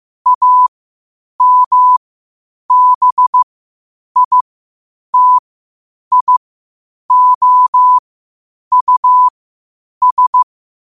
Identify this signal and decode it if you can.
Morse code, heard as a keyed tone, about 15 wpm, spaced as for 8 wpm, 1000 Hz. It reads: AMBITIOUS
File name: qotdfiles / 8wpm